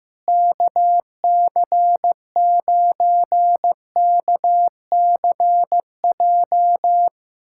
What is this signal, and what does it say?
Morse code, heard as a keyed tone, about 15 wpm, 700 Hz. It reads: KC9KCJ
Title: USA Callsign DE Callsign 15wpm